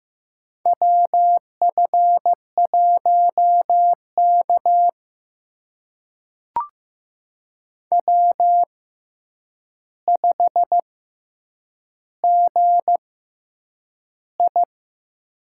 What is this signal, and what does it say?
Morse code, heard as a keyed tone, about 15 wpm, spaced as for 8 wpm, 700 Hz. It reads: WF1K W5GI
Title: USA Call Signs - 4 Characters - 6x Character Spacing - 15wpm